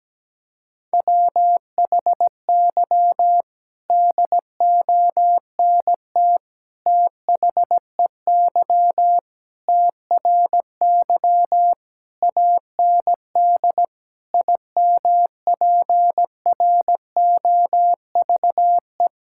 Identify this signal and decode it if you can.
Morse code, heard as a keyed tone, about 17 wpm, 700 Hz. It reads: WHY DONT THEY TRY AND IMPROVE
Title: Sentences from Top 1000 Words - Review Part 5 - 17wpm